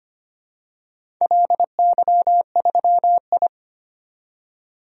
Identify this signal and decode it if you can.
Morse code, heard as a keyed tone, about 25 wpm, 700 Hz. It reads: LY3I
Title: DXCC-100 Call Signs - 4 Characters - 25wpm